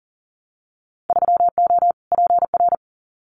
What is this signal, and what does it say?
Morse code, heard as a keyed tone, about 38 wpm, 700 Hz. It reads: 3O PR
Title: Winter Field Day Exchanges Encore 40wpm